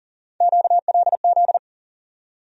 Morse code, heard as a keyed tone, about 40 wpm, 700 Hz. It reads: QLZ
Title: USA Call Sign Postfixes - 3 Letters 40wpm